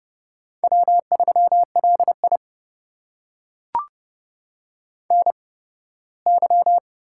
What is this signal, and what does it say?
Morse code, heard as a keyed tone, about 30 wpm, spaced as for 18 wpm, 700 Hz. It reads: W3LI NY
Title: USA Call Signs - 4 Characters - 8x Character Spacing - 30wpm